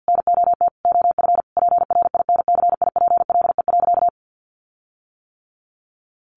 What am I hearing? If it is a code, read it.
Morse code, heard as a keyed tone, about 50 wpm, 700 Hz. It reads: NOT OF PRINCIPLE.
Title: Sentences from Top 1400 Words 50wpm